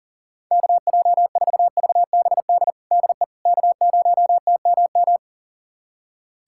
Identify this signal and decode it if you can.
Morse code, heard as a keyed tone, about 40 wpm, 700 Hz. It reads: KJ4VBD DE K0TMM
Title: USA Callsign DE Callsign 40wpm